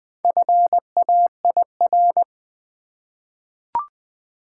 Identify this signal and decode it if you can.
Morse code, heard as a keyed tone, about 20 wpm, 700 Hz. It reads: FAIR
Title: Four Letter Words - 20wpm